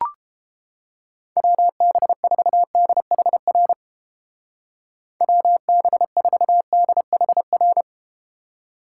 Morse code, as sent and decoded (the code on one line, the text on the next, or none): .-- -... ....- -.. .... .-. / .-- -... ....- -.. .... .-.
WB4DHR WB4DHR